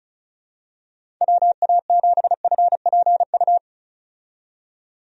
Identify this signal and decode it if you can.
Morse code, heard as a keyed tone, about 34 wpm, 700 Hz. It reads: WA7FPU
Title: USA Call Signs - 6 Characters - 35wpm